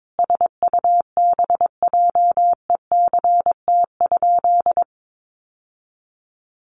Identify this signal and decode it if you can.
Morse code, heard as a keyed tone, about 22 wpm, 700 Hz. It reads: SUBJECT?
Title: Sentences from Top 800 Words 22wpm